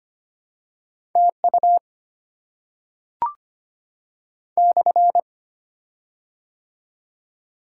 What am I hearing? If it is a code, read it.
Morse code, heard as a keyed tone, about 25 wpm, 700 Hz. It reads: TU /
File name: Top 90 QSO Elements - 6x Character Spacing - 25wpm